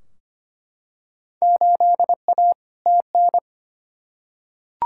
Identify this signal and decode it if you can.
Morse code, heard as a keyed tone, about 25 wpm, 700 Hz. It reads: 8A TN
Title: Field Day Exchanges Encore - 25wpm